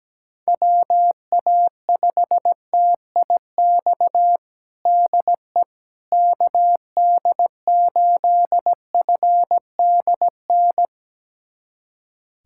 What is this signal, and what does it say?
Morse code, heard as a keyed tone, about 17 wpm, 700 Hz. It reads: WA5TIX DE KD8FDN
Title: USA Callsign DE Callsign - 6 Characters - 17wpm